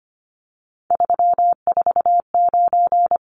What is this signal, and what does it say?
Morse code, heard as a keyed tone, about 25 wpm, 700 Hz. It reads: 349
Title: Numbers - 3-Digit 25wpm